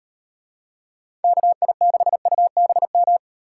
Morse code, heard as a keyed tone, about 35 wpm, 700 Hz. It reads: KI6UBM